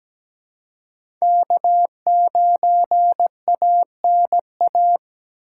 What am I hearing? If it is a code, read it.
Morse code, heard as a keyed tone, about 17 wpm, 700 Hz. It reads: K9ANA
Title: USA Call Signs - 5 Characters - 3x Character Spacing - 17wpm